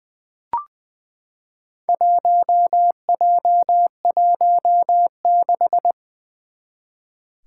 Morse code, heard as a keyed tone, about 20 wpm, 700 Hz. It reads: 1J16
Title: Four Challenging Letter-Number Combinations 20wpm